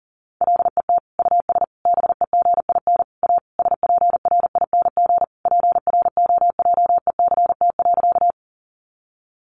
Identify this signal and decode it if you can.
Morse code, heard as a keyed tone, about 40 wpm, 700 Hz. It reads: LET US BEGIN A SPRING PROJECT.
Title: Sentences from Top 1400 Words 40wpm